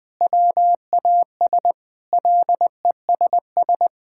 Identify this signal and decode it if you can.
Morse code, heard as a keyed tone, about 20 wpm, 700 Hz. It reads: WAS LESS
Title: Sentences from Top 1600 Words 20wpm